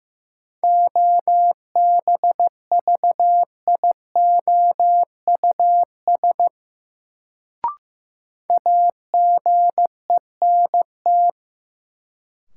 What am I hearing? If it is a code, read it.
Morse code, heard as a keyed tone, about 16 wpm, 700 Hz. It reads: OBVIOUS AGENT
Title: Sentences from Top 1600 Words - New Word Drill Alt - 15wpm